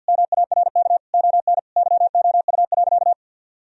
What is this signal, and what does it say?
Morse code, heard as a keyed tone, about 52 wpm, 700 Hz. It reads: MARK ON YOU.
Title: Sentences from Top 1000 Words - Review Part 1 - 50wpm